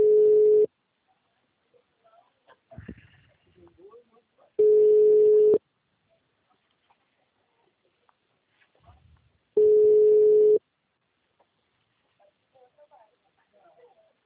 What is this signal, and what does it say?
Ringback tone (Europe, 425 Hz), with speech.